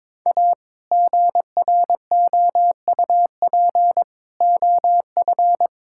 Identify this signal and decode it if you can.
Morse code, heard as a keyed tone, about 22 wpm, 700 Hz. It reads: A GROUP OF
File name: practice / 22wpm / Sentences with 3-Word Phrases - 22wpm